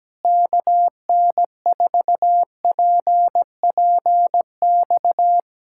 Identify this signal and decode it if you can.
Morse code, heard as a keyed tone, about 17 wpm, 700 Hz. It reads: KN4PPX
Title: Speed-Racing Common QSO Phrases - 17wpm